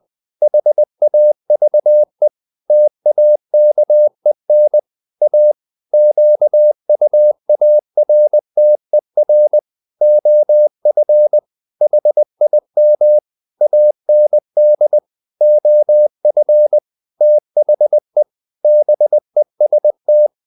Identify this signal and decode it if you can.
Morse code, heard as a keyed tone, about 20 wpm, 600 Hz. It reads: HAVE TAKEN A QUARTER OF HIM AND OF THE BEST
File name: the old man and the sea part 2 20wpm